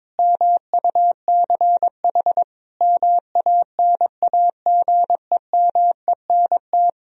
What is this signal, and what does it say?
Morse code, heard as a keyed tone, about 21 wpm, 700 Hz. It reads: MUCH MANAGEMENT
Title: Sentences from Top 1000 Words - Review Part 4 - 22wpm